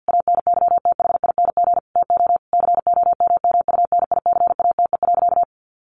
Morse code, heard as a keyed tone, about 50 wpm, 700 Hz. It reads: ANYTHING TO COMMUNICATE.
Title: Sentences from Top 1300 Words 50wpm